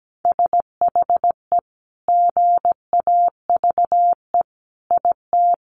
Morse code, heard as a keyed tone, about 17 wpm, 700 Hz. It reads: SHE GAVE IT
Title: Sentences from Top 1500 Words 17wpm